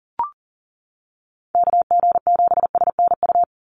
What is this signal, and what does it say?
Morse code, heard as a keyed tone, about 40 wpm, 700 Hz. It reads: KG7SNU
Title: USA Call Signs 40wpm